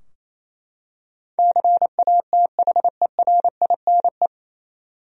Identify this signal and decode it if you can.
Morse code, heard as a keyed tone, about 28 wpm, 700 Hz. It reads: CATHERINE
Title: USA Common Names - 28wpm